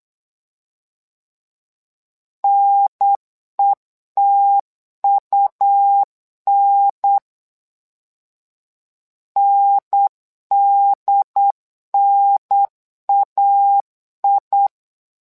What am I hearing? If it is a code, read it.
Morse code, heard as a keyed tone, about 8 wpm, 800 Hz. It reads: NETUN NDNAI
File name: LEZIONI / 2 UHDN / 2 UHDN 8wpm2232